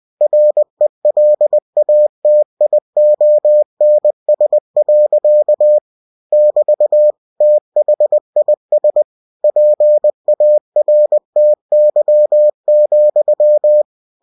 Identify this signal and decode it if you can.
Morse code, heard as a keyed tone, about 20 wpm, 600 Hz. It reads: RELATIONS. <BT> THIS PARTY,
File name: cw-books / winnie-the-pooh / winnie-20wpm-page-0107